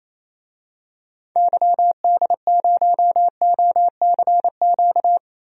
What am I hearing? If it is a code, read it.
Morse code, heard as a keyed tone, about 28 wpm, 700 Hz. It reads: YD0OCQ
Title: DXCC-100 Call Signs - 6 Characters - Repeated 3x - 28wpm